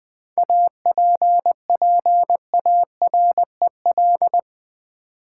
Morse code, heard as a keyed tone, about 20 wpm, 700 Hz. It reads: APPAREL
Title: 7 Letter Words - First 3 Letters Given - 20wpm